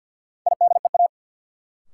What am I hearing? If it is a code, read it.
Morse code, heard as a keyed tone, about 53 wpm, 700 Hz. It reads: IDEA
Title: Sets of 3 Words - New Word Drill - 50wpm